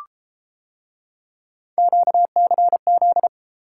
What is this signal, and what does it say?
Morse code, heard as a keyed tone, about 32 wpm, 700 Hz. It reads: QCZ